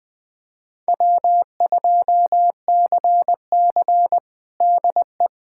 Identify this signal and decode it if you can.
Morse code, heard as a keyed tone, about 20 wpm, 700 Hz. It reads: W2CC DE
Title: USA Callsign DE Callsign - 4 Characters - 20wpm